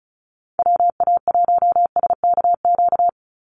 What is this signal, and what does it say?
Morse code, heard as a keyed tone, about 34 wpm, 700 Hz. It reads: WA1SKQ
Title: Call Signs Repeated 2x - 35wpm